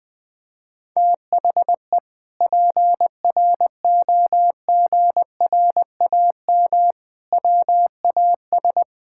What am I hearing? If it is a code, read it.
Morse code, heard as a keyed tone, about 20 wpm, 700 Hz. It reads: THE PROGRAM WAS
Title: Sentences from Top 1000 Words - Review Part 5 - 20wpm